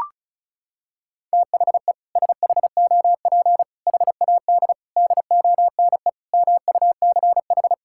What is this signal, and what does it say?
Morse code, heard as a keyed tone, about 34 wpm, 700 Hz. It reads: THE SHOP HAD DONE MUCH
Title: Sentences from Top 1600 Words 35wpm